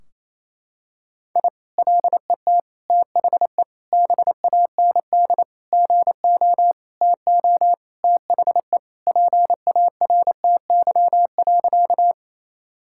Morse code, heard as a keyed tone, about 28 wpm, 700 Hz. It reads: I LET THE BAND GO TO THE PARTY.
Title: Sentences from Top 1600 Words 28wpm